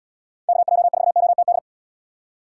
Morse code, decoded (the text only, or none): KJ4OTD